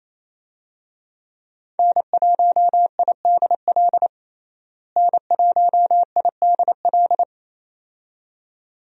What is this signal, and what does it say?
Morse code, heard as a keyed tone, about 27 wpm, 700 Hz. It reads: N1IDL N1IDL
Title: USA Call Signs - 5 Characters - Repeated 2x - 28wpm